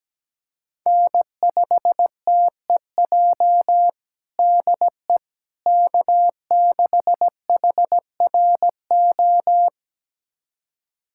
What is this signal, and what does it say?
Morse code, heard as a keyed tone, about 17 wpm, 700 Hz. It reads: N5TEJ DE K6HRO
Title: USA Callsign DE Callsign 17wpm